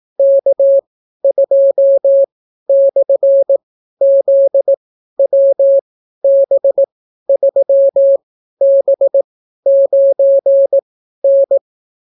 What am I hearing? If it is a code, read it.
Morse code, heard as a keyed tone, about 18 wpm, spaced as for 12 wpm, 550 Hz. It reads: K2/ZWB3B9N